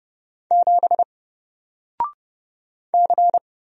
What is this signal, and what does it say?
Morse code, heard as a keyed tone, about 30 wpm, 700 Hz. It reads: 7 C
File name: Single-Letter-Number-30wpm